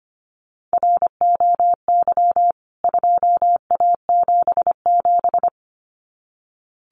Morse code, heard as a keyed tone, about 25 wpm, 690 Hz. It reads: ROY 2A77
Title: CWT Exchange 25wpm